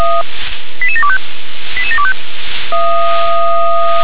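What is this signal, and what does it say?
Five-tone selective call (ZVEI), heard as quick runs of five tones, tones of about 70 ms. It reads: 88725 88725
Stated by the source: Five-tone ZVEI selective call received on an SDR.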